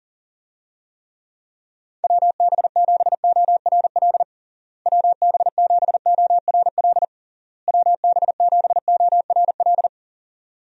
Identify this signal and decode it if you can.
Morse code, heard as a keyed tone, about 40 wpm, 700 Hz. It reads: WB7ORL WB7ORL WB7ORL
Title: USA Call Signs - 6 Characters - Repeated 3x - 40wpm